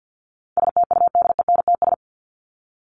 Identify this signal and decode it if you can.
Morse code, heard as a keyed tone, about 50 wpm, 700 Hz. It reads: STUDENTS
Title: ES:S Suffix - Top 100 Words with Base Words 50wpm